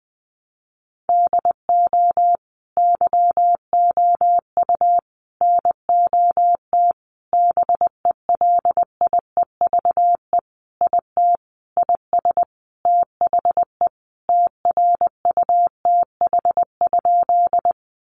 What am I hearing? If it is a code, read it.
Morse code, heard as a keyed tone, about 20 wpm, 700 Hz. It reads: DO YOU NOT BELIEVE IT IS THE TRUTH?
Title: Sentences from Top 1000 Words 20wpm